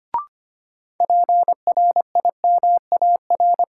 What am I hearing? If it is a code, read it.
Morse code, heard as a keyed tone, about 25 wpm, 700 Hz. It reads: PRIMAR
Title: Sentences from Top 1000 Words - Review Part 5 - New Word Drill - 25wpm